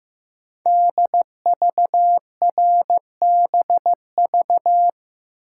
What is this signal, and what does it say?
Morse code, heard as a keyed tone, about 15 wpm, 700 Hz. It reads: DVRBV
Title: Code Groups - 5 Letters - 15wpm